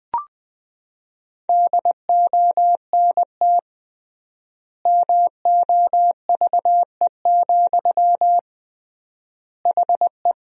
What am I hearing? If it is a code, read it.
Morse code, heard as a keyed tone, about 20 wpm, 700 Hz. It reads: DONT MOVE, HE
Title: 3x Word-Spacing - Sentences from Top 300 Words - 20wpm